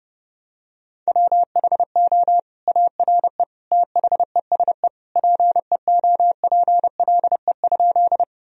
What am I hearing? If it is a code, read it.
Morse code, heard as a keyed tone, about 30 wpm, 700 Hz. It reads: WHO ARE THESE PEOPLE?
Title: Sentences from Top 1000 Words - Review Part 1 - 30wpm